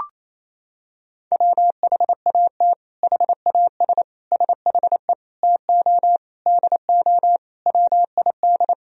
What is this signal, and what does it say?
Morse code, heard as a keyed tone, about 28 wpm, 700 Hz. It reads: WHAT HAS SHE TO DO WID
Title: Sentences from Top 100 Words and Abbreviations 28wpm